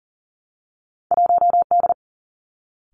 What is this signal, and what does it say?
Morse code, heard as a keyed tone, about 38 wpm, 700 Hz. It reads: 1 D
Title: Two Challenging Letter-Number Combinations 40wpm